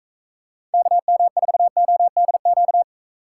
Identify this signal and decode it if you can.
Morse code, heard as a keyed tone, about 43 wpm, 700 Hz. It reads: KM4ODQ